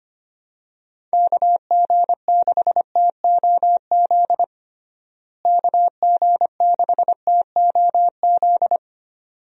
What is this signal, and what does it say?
Morse code, heard as a keyed tone, about 25 wpm, 700 Hz. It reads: KG6TOZ KG6TOZ
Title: USA Call Signs - 6 Characters - Repeated 2x - 25wpm